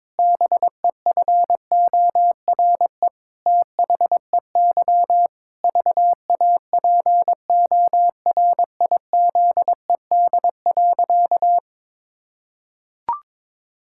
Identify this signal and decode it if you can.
Morse code, heard as a keyed tone, about 22 wpm, 700 Hz. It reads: BEFORE THEY VAPORIZED.
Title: Pangram Challenge v2 - 22wpm